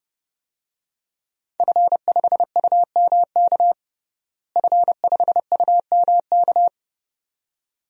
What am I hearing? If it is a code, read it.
Morse code, heard as a keyed tone, about 30 wpm, 700 Hz. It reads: F5UMK F5UMK
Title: DXCC-100 Call Signs - 5 Characters - Repeated 2x - 30wpm